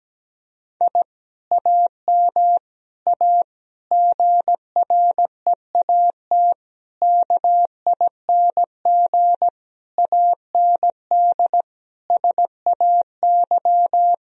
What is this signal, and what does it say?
Morse code, heard as a keyed tone, about 17 wpm, 700 Hz. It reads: I AM A GREAT KING AND SAY
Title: Sentences from Top 1000 Words - Review Part 4 - 17wpm